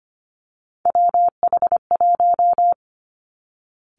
Morse code, heard as a keyed tone, about 25 wpm, 700 Hz. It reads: WH1
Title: Three Challenging Letter-Number Combinations 25wpm